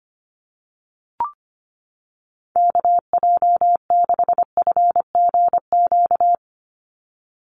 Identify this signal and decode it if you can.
Morse code, heard as a keyed tone, about 25 wpm, 700 Hz. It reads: KJ6FGQ
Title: USA Call Signs 25wpm